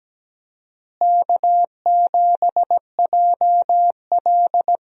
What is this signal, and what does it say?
Morse code, heard as a keyed tone, about 17 wpm, 700 Hz. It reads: K7JL
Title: USA Call Signs - 4 Characters - Repeated 2x - 17wpm